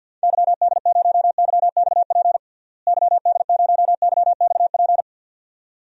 Morse code, heard as a keyed tone, about 52 wpm, 700 Hz. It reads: YD0YXP YD0YXP
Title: DXCC-100 Call Signs - 6 Characters - Repeated 2x - 50wpm